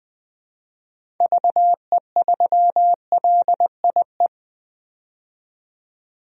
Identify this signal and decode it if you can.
Morse code, heard as a keyed tone, about 21 wpm, 700 Hz. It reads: VE3LIE
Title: DXCC-100 Call Signs - 6 Characters - 20wpm